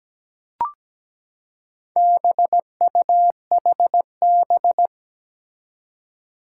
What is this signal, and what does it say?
Morse code, heard as a keyed tone, about 17 wpm, 700 Hz. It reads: BUHB
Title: Four Challenging Letter-Number Combinations 17wpm